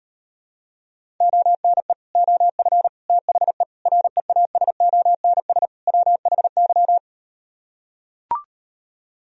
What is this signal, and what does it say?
Morse code, heard as a keyed tone, about 38 wpm, 700 Hz. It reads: ONE OF THE REASONS WHY